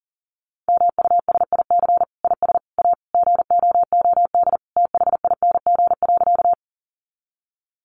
Morse code, heard as a keyed tone, about 40 wpm, 700 Hz. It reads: MUSIC IS A GOOD THING.
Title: Sentences from Top 300 Words 40wpm